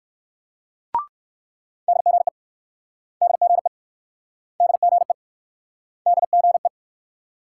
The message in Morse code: DGE DGE DGE DGE

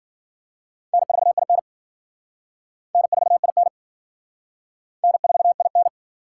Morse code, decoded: N4IN N4IN N4IN

